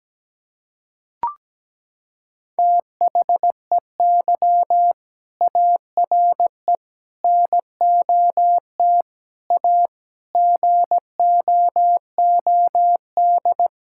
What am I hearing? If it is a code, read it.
Morse code, heard as a keyed tone, about 17 wpm, 700 Hz. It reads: THEY ARE NOT A GOOD
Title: Sentences from Top 1600 Words 17wpm